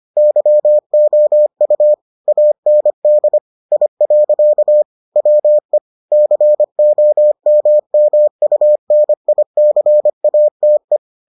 Morse code, heard as a keyed tone, about 25 wpm, 600 Hz. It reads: YOU AND I. WE COMMUNICATE